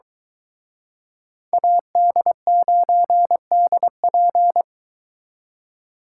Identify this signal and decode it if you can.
Morse code, heard as a keyed tone, about 23 wpm, 700 Hz. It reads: AD9DP